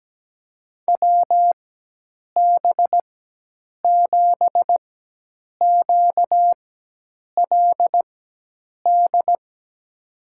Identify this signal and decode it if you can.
Morse code, heard as a keyed tone, about 17 wpm, spaced as for 8 wpm, 700 Hz. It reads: WB7QLD
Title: USA Call Signs - 6 Characters - 4x Character Spacing - 17wpm